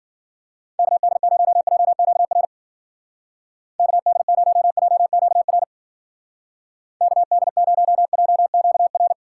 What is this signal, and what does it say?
Morse code, heard as a keyed tone, about 49 wpm, 700 Hz. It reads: KD0JQR KD0JQR KD0JQR
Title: Speed Racing - USA 2x3 Call Signs 40wpm